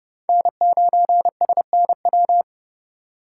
Morse code, heard as a keyed tone, about 30 wpm, 700 Hz. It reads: N9SNW
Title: USA Call Signs - 5 Characters - 2x Character Spacing - 30wpm